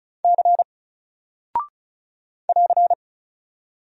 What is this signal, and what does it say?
Morse code, heard as a keyed tone, about 36 wpm, 700 Hz. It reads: C <AR>
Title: Single Character (Letter:Number:Punctuation:Prosign) Alt 35wpm